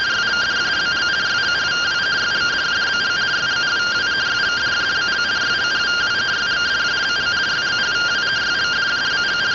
Two-tone teleprinter signal, about 170 Hz shift.